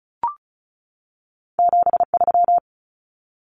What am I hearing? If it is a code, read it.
Morse code, heard as a keyed tone, about 36 wpm, 700 Hz. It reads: 73
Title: Numbers - 2-Digit 35wpm